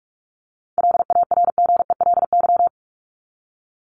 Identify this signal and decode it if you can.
Morse code, heard as a keyed tone, about 44 wpm, 700 Hz. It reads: LARGELY
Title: LY Suffix - Top 100 Words with Base Words 45wpm